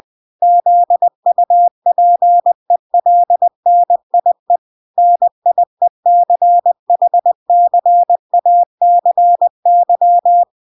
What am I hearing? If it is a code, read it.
Morse code, heard as a keyed tone, about 20 wpm, 700 Hz. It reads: ZUPELNIE NIECHCACY